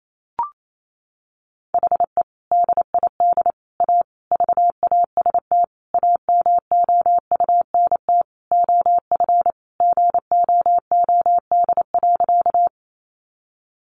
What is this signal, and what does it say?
Morse code, heard as a keyed tone, about 28 wpm, 700 Hz. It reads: HE DID A VAST AMOUNT OF GOOD.
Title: Sentences from Top 1300 Words 28wpm